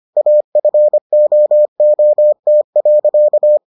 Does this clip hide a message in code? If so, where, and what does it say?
Yes: Morse code from 0.2 to 3.6 s, heard as a keyed tone, about 25 wpm, 600 Hz. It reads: AFOOT.